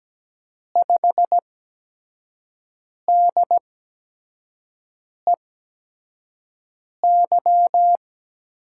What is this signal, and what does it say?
Morse code, heard as a keyed tone, about 17 wpm, spaced as for 5 wpm, 700 Hz. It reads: 5DEY
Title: USA Call Signs - 6 Characters - 8x Character Spacing - 17wpm